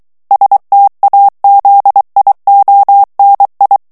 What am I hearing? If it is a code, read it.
Morse code, heard as a keyed tone, about 23 wpm, 800 Hz. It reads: STAZIONI
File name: ITALIEN texte clair 2 vitesse 22 mots minute